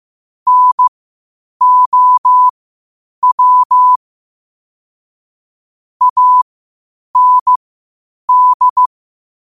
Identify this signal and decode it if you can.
Morse code, heard as a keyed tone, about 15 wpm, spaced as for 8 wpm, 1000 Hz. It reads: NOW AND
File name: qotdfiles / 8wpm